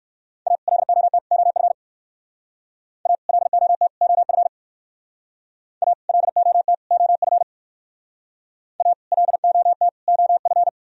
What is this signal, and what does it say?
Morse code, heard as a keyed tone, about 57 wpm, 700 Hz. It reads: A LOT OF A LOT OF A LOT OF A LOT OF